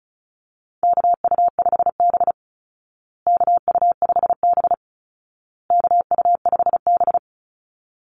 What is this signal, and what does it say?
Morse code, heard as a keyed tone, about 34 wpm, 700 Hz. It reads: KU5B KU5B KU5B
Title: Call Signs Repeated 2x - 35wpm